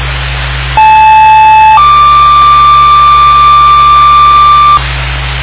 Two-tone page, heard as a one-second tone, then a three-second tone, about 870 then 1180 Hz, with noise. It is heard from 0.8 to 4.8 s.